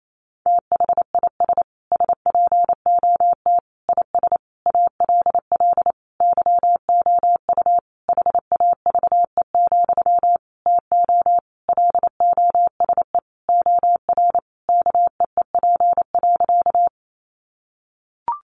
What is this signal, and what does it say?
Morse code, heard as a keyed tone, about 28 wpm, 700 Hz. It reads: THIS SPOT IS ALL YOU HAVE, TO LOSE OR KEEP.
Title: Sentences from Top 1100 Words 28wpm